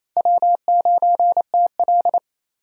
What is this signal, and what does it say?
Morse code, heard as a keyed tone, about 27 wpm, 700 Hz. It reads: W9TL
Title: USA Call Signs - 4 Characters - 4x Character Spacing - 28wpm